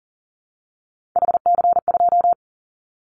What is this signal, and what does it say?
Morse code, heard as a keyed tone, about 41 wpm, 700 Hz. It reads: HC2
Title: Callsign Prefixes for Top 100 DXCC - 40wpm